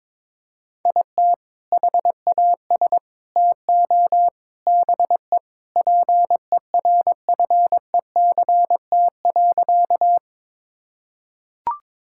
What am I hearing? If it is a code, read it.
Morse code, heard as a keyed tone, about 22 wpm, 700 Hz. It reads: IT HAS TO BE PERFECT.
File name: practice / 22wpm / Sentences with 4-Word Phrases - 22wpm